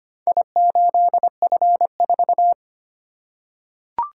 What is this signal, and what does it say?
Morse code, heard as a keyed tone, about 25 wpm, 700 Hz. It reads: I8F4
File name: practice / 25wpm / Four Letter-Number Combinations - 25wpm